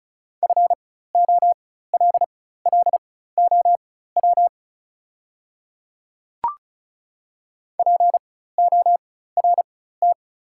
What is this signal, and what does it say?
Morse code, heard as a keyed tone, about 34 wpm, spaced as for 16 wpm, 690 Hz. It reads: FOLLOW PORT
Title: 4x Character Spacing - Top 200 Words - 35wpm